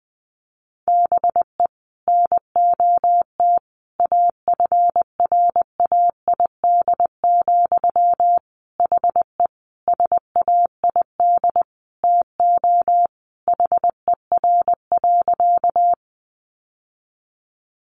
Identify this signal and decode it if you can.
Morse code, heard as a keyed tone, about 20 wpm, 700 Hz. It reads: BE NOT AFRAID, HE SAID TO HER.
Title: Sentences from Top 1500 Words 20wpm